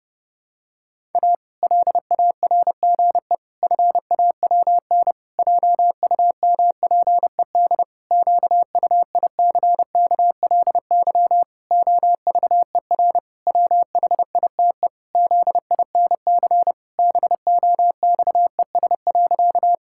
Morse code, heard as a keyed tone, about 30 wpm, 700 Hz. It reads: A LARGE FAWN JUMPED QUICKLY OVER WHITE ZINC BOXES.
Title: Pangram Challenge v2 - 30wpm